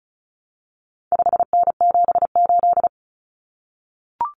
Morse code, heard as a keyed tone, about 34 wpm, 700 Hz. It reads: 5N78
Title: Common Cut Numbers - 4-Digit 35wpm